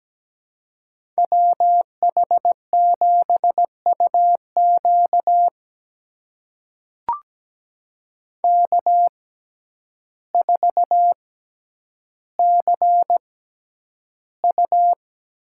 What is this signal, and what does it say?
Morse code, heard as a keyed tone, about 17 wpm, spaced as for 9 wpm, 700 Hz. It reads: WH7UQ K4CU
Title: USA Call Signs - 6x Character Spacing - 17wpm